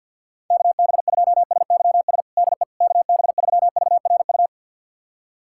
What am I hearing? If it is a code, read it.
Morse code, heard as a keyed tone, about 51 wpm, 700 Hz. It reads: KB2SYS DE KB3VRU
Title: USA Callsign DE Callsign - 6 Characters - 50wpm